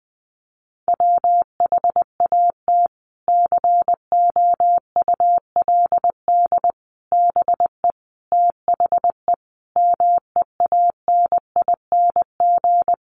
Morse code, heard as a keyed tone, about 20 wpm, 700 Hz. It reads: WHAT COULD BE THE MEANING
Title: Sentences from Top 900 Words 20wpm